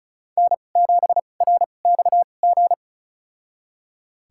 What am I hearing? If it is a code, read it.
Morse code, heard as a keyed tone, about 35 wpm, spaced as for 25 wpm, 700 Hz. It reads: N7RXG
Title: USA Call Signs - 5 Characters - 2x Character Spacing - 35wpm